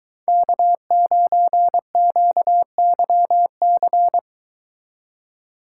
Morse code, heard as a keyed tone, about 23 wpm, 700 Hz. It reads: K9QYC